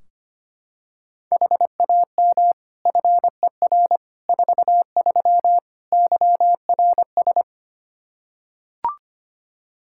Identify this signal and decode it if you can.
Morse code, heard as a keyed tone, about 25 wpm, 700 Hz. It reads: HAM FER 43 YRS
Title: Practice QSO - Exchange 3 - More Info - 25wpm